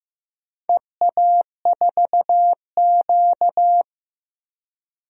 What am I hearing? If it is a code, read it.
Morse code, heard as a keyed tone, about 15 wpm, 700 Hz. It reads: EA4Q
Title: DXCC-100 Call Signs - 4 Characters - Repeated 2x - 15wpm